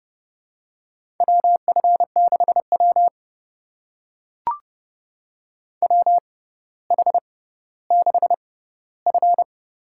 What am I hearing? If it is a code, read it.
Morse code, heard as a keyed tone, about 30 wpm, spaced as for 15 wpm, 700 Hz. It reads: WF6W WH6F
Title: USA Call Signs - 4 Characters - 6x Character Spacing - 30wpm